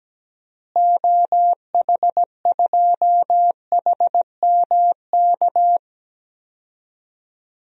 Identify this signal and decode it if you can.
Morse code, heard as a keyed tone, about 17 wpm, 700 Hz. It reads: OH2HMK
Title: DXCC-100 Call Signs - 6 Characters - Repeated 3x - 17wpm